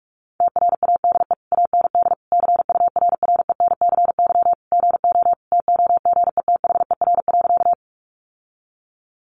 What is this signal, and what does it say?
Morse code, heard as a keyed tone, about 44 wpm, 700 Hz. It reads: TRADE AND CURRENCY GO TOGETHER.
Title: Sentences from Top 1500 Words 45wpm